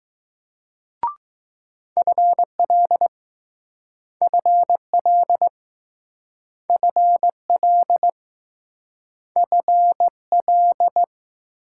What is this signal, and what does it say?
Morse code, heard as a keyed tone, about 20 wpm, 700 Hz. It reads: FL FL FL F L